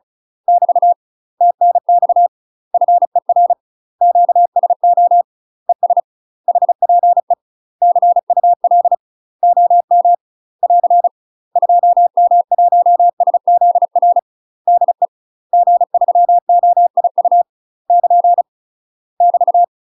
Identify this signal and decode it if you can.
Morse code, heard as a keyed tone, about 34 wpm, 700 Hz. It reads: <BT> TNX FER QSO ES HPE CUL OM <AR> 2M1SZR DE G3OIU <KN> <BT>
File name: UK MASTER QSOs TRAINING / 35WPM / UK MASTER QSOs 0025